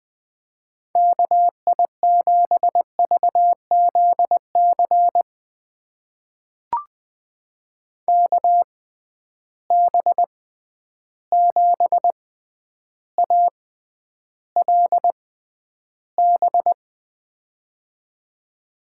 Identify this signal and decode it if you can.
Morse code, heard as a keyed tone, about 20 wpm, spaced as for 10 wpm, 700 Hz. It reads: KI7VZC KB7ALB
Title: USA Call Signs - 6 Characters - 6x Character Spacing - 20wpm